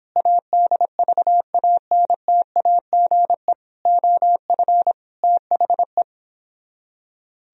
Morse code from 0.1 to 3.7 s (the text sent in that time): ADVANTAGE